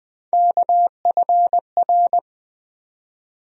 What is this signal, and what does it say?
Morse code, heard as a keyed tone, about 20 wpm, 700 Hz. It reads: KFR
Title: USA Call Sign Postfixes - 3 Letters 20wpm